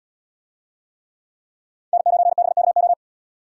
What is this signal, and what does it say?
Morse code, heard as a keyed tone, about 71 wpm, 700 Hz. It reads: N9BKO